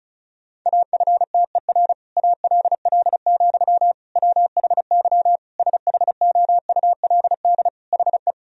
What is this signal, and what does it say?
Morse code, heard as a keyed tone, about 34 wpm, 700 Hz. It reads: AFTER ALL, WHY SHOULD HE
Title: Sentences from Top 1000 Words - Review Part 2 - 35wpm